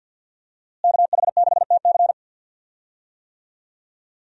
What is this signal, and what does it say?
Morse code, heard as a keyed tone, about 48 wpm, 700 Hz. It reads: KH6TC
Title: DXCC-100 Call Signs - 5 Characters - Repeated 2x - 50wpm